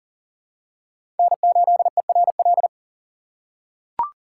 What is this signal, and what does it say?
Morse code, heard as a keyed tone, about 39 wpm, 700 Hz. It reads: N8ERL
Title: USA Call Signs - 5 Characters - 2x Character Spacing - 40wpm